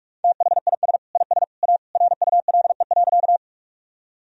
Morse code, heard as a keyed tone, about 46 wpm, 700 Hz. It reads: THIS IS A RULE.
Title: Sentences from Top 1000 Words - Review Part 2 - 45wpm